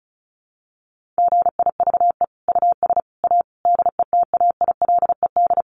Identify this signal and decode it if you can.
Morse code, heard as a keyed tone, about 34 wpm, 700 Hz. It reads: GIVE US A DETAILED
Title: Sentences from Top 1500 Words 35wpm